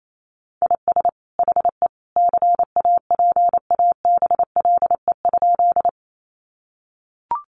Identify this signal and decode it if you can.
Morse code, heard as a keyed tone, about 28 wpm, 700 Hz. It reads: IS HE CAPABLE?
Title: Sentences from Top 1200 Words 28wpm